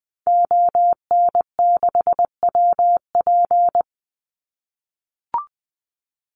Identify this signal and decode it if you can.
Morse code, heard as a keyed tone, about 20 wpm, 700 Hz. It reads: ON6WP
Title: Call Signs - 20wpm